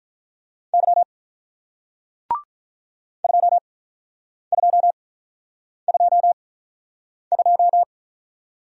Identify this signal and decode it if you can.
Morse code, heard as a keyed tone, about 48 wpm, spaced as for 9 wpm, 700 Hz. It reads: Y 2222